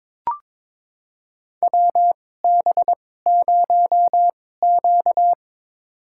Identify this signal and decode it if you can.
Morse code, heard as a keyed tone, about 22 wpm, spaced as for 16 wpm, 700 Hz. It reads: WB0Q
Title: USA Call Signs - 4 Characters - 2x Character Spacing - 22wpm